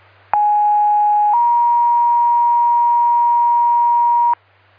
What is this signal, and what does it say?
Two-tone page, heard as a one-second tone, then a three-second tone, about 830 then 950 Hz. It is heard from 0.3 to 4.3 s.